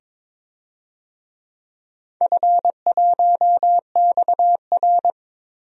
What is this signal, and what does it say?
Morse code, heard as a keyed tone, about 22 wpm, 700 Hz. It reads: F1XR
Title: DXCC-100 Call Signs - 4 Characters - 22wpm